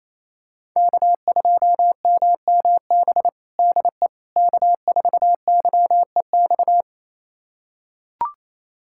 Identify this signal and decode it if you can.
Morse code, heard as a keyed tone, about 27 wpm, 700 Hz. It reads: K2MMB DE K4YEX
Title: USA Callsign DE Callsign - 5 Characters - 28wpm